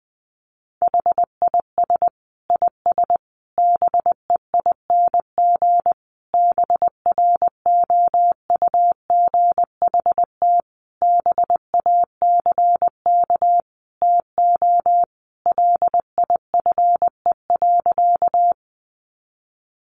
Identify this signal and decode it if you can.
Morse code, heard as a keyed tone, about 20 wpm, 700 Hz. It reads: HIS IS BEING BROUGHT BACK TO LIFE.
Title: Sentences from Top 600 Words 20wpm